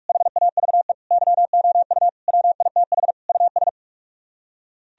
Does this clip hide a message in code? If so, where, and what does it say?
Yes: Morse code from 0.1 to 3.7 s, heard as a keyed tone, about 44 wpm, 700 Hz. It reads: HAVE YOU WITH US